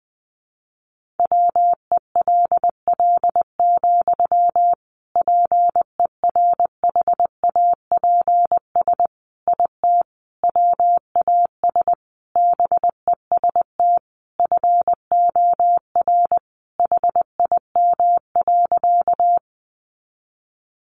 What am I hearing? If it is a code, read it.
Morse code, heard as a keyed tone, about 20 wpm, 700 Hz. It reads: WELL, PERHAPS IT WAS BEST FOR HIM.
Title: Sentences from Top 500 Words 20wpm